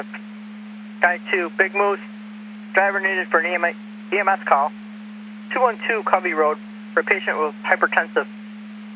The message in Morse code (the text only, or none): none